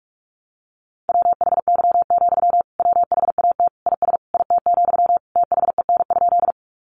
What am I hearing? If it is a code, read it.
Morse code, heard as a keyed tone, about 44 wpm, 700 Hz. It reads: WHY, WHAT IS IT, THEN?
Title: Sentences from Top 200 Words 45wpm